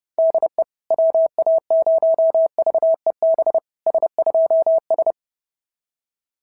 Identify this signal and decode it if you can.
Morse code, heard as a keyed tone, about 30 wpm, 650 Hz. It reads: DE WA0VEB S2S
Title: SOTA QSO Practice 30wpm